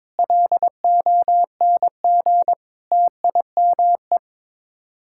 Morse code, heard as a keyed tone, about 22 wpm, 700 Hz. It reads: LONG TIME
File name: practice / 22wpm / Sets of 8 Words - 22wpm